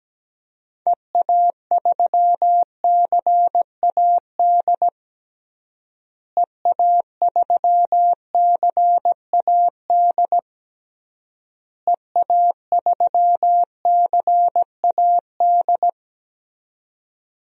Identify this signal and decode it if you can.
Morse code, heard as a keyed tone, about 17 wpm, 700 Hz. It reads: EA3CAD EA3CAD EA3CAD
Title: DXCC-100 Call Signs - 6 Characters - Repeated 3x - 17wpm